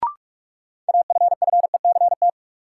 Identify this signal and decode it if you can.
Morse code, heard as a keyed tone, about 46 wpm, 700 Hz. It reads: AFFECT
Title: Sentences from Top 900 Words - New Word Drill - 45wpm